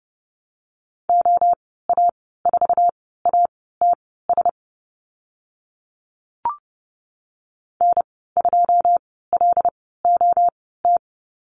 Morse code, heard as a keyed tone, about 30 wpm, spaced as for 17 wpm, 700 Hz. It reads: OA4ATS N2LOT
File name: Lesson 15g - Number 5 - Call Signs 30wpm